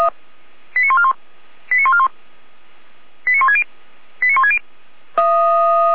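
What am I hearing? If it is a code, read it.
Five-tone selective call (ZVEI), heard as quick runs of five tones, tones of about 70 ms. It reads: 89141 89141 89160 89160
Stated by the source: ZVEI five-tone call received by SDR.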